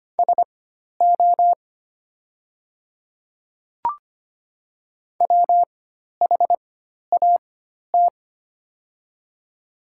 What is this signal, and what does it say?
Morse code, heard as a keyed tone, about 25 wpm, spaced as for 12 wpm, 690 Hz. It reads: SO WHAT
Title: 4x Character Spacing - Top 500 Words - 25wpm